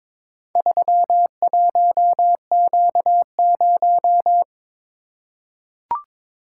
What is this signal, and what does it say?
Morse code, heard as a keyed tone, about 22 wpm, 700 Hz. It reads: 31Q0
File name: practice / 22wpm / Four Letter-Number Combinations - 22wpm